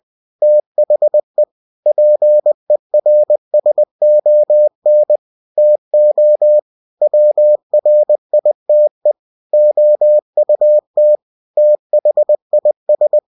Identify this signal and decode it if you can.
Morse code, heard as a keyed tone, about 20 wpm, 600 Hz. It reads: THE PERSON TO WRITE OUT THIS
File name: winnie-20wpm